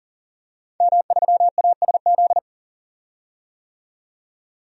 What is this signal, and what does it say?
Morse code, heard as a keyed tone, about 40 wpm, 700 Hz. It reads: M3ASZ
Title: DXCC-100 Call Signs - 5 Characters - 40wpm